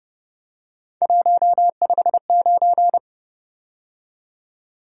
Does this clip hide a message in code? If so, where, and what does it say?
Yes: Morse code from 1.0 to 3.0 s, heard as a keyed tone, about 29 wpm, 700 Hz. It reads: 159